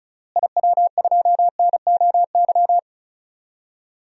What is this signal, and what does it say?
Morse code, heard as a keyed tone, about 36 wpm, 700 Hz. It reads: IW2NOY
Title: DXCC-100 Call Signs - 6 Characters - Repeated 2x - 35wpm